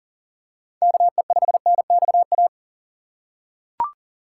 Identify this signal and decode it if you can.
Morse code, heard as a keyed tone, about 40 wpm, 700 Hz. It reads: KE5NXA
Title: USA Call Signs - 6 Characters - 3x Character Spacing - 40wpm